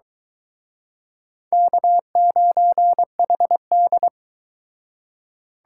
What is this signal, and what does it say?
Morse code, heard as a keyed tone, about 23 wpm, 700 Hz. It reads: K9HD